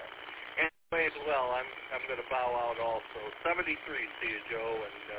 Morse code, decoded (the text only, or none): none